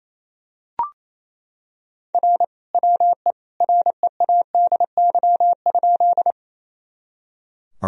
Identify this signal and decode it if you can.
Morse code, heard as a keyed tone, about 28 wpm, 700 Hz. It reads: R WE READY?
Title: Sentences from Top 400 Words and Abbreviations 28wpm